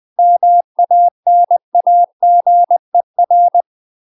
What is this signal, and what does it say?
Morse code, heard as a keyed tone, about 20 wpm, 700 Hz. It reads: MANAGER
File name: UK MASTER QSOs TRAINING / 20WPM / UK MASTER QSOs 0026